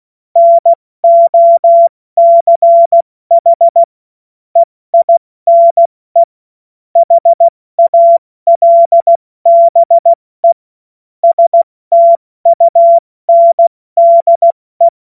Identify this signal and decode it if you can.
Morse code, heard as a keyed tone, about 16 wpm, spaced as for 14 wpm, 680 Hz. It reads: NOCH EINE HALBE STUNDE
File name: Witz-23-September-2025-70-F80